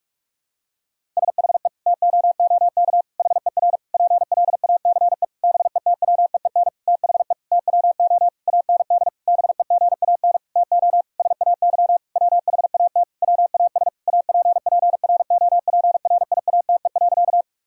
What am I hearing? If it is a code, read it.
Morse code, heard as a keyed tone, about 43 wpm, 700 Hz. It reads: SHE TOOK HER PLACE BETWEEN THE TWO AND BEGAN TO SAY WHAT WAS APPROPRIATE.
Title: Sentences from Top 1000 Words - Review Part 5 - 45wpm